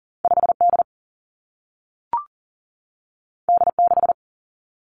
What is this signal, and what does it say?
Morse code, heard as a keyed tone, about 40 wpm, 700 Hz. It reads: 5 D D 6
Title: Two Challenging Letter-Number Combinations 40wpm